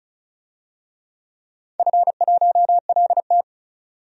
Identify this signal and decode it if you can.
Morse code, heard as a keyed tone, about 36 wpm, 690 Hz. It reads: F1LT
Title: DXCC-100 Call Signs - 4 Characters - 35wpm